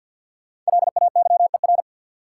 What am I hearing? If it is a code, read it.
Morse code, heard as a keyed tone, about 53 wpm, 700 Hz. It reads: LAYER
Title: Sentences from Top 1600 Words - New Word Drill Alt - 50wpm